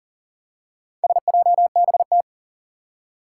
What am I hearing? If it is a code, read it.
Morse code, heard as a keyed tone, about 40 wpm, 700 Hz. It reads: SJBT
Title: Four Challenging Letter-Number Combinations 40wpm